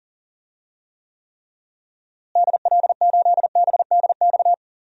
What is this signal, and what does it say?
Morse code, heard as a keyed tone, about 40 wpm, 700 Hz. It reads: DL8BDX
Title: DXCC-100 Call Signs - 6 Characters - Repeated 3x - 40wpm